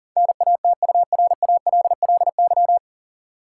Morse code, heard as a keyed tone, about 40 wpm, 700 Hz. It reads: NATURALLY
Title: Sentences from Top 1300 Words - New Word Drill - 40wpm